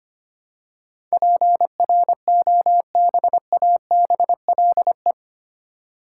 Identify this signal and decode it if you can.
Morse code, heard as a keyed tone, about 25 wpm, 700 Hz. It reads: PROBABLE
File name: Eight Letter Words - 25wpm